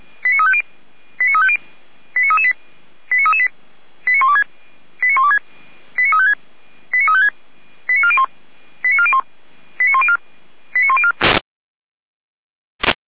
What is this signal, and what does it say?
Five-tone selective call (ZVEI), heard as quick runs of five tones, tones of about 70 ms. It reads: 89350 89350 89338 89338 89126 89126 89356 89356 89401 89401 89104 89104
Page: SDR/ZVEI